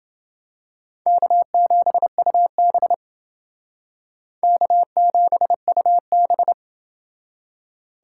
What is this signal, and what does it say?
Morse code, heard as a keyed tone, about 29 wpm, 700 Hz. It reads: K7UB K7UB